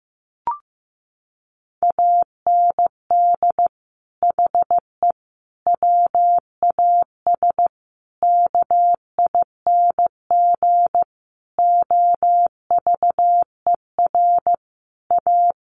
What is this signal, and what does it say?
Morse code, heard as a keyed tone, about 15 wpm, 700 Hz. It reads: AND HE WAS KING OVER A
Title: Sentences from Top 1400 Words 15wpm